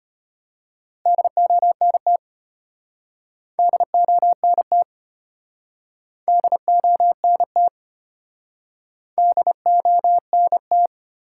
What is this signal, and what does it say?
Morse code, heard as a keyed tone, about 33 wpm, 700 Hz. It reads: DONT DONT DONT DONT